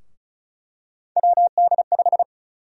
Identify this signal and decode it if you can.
Morse code, heard as a keyed tone, about 36 wpm, 700 Hz. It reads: WD5
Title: USA Call Sign Prefixes - Repeated 2x - 35wpm